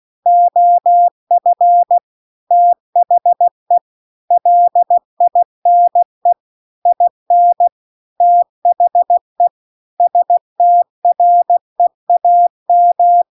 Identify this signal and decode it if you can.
Morse code, heard as a keyed tone, about 16 wpm, 700 Hz. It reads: OF THE LINE IN THE STREAM
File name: CW-The-Big-Twoheaded-River-2-16WPM